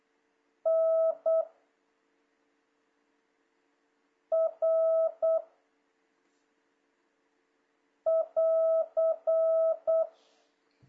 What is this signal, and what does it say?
Morse code, heard as a keyed tone, about 8 wpm, 650 Hz. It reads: N R <AR>
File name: jh8cbh / 2026cw